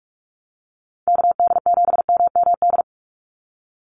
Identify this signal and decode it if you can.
Morse code, heard as a keyed tone, about 46 wpm, 700 Hz. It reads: KD7MMD
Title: USA Call Signs 45wpm